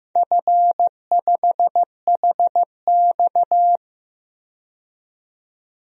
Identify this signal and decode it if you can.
Morse code, heard as a keyed tone, about 15 wpm, 700 Hz. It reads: F5HX
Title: DXCC-100 Call Signs - 4 Characters - 15wpm